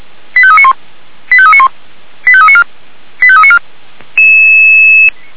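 Five-tone selective call (ZVEI), heard as quick runs of five tones, tones of about 70 ms. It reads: 85391 85391 85394 85394
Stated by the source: ZVEI five-tone call 89138 received on SDR.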